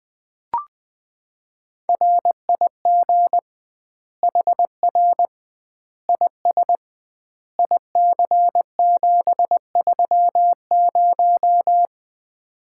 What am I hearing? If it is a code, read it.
Morse code, heard as a keyed tone, about 21 wpm, 700 Hz. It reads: RIG HR IS IC730
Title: Common QSO Phrases - 2x Word-Spacing - 20wpm